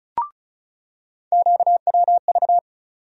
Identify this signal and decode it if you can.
Morse code, heard as a keyed tone, about 34 wpm, 700 Hz. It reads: QWV
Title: USA Call Sign Postfixes - 3 Letters 35wpm